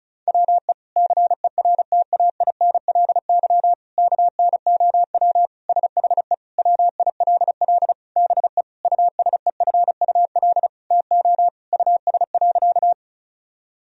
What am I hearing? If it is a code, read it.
Morse code, heard as a keyed tone, about 34 wpm, 700 Hz. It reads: WE CERTAINLY KNOW SHE WILL BE USEFUL TO US.
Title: Sentences from Top 1000 Words - Review Part 4 - 35wpm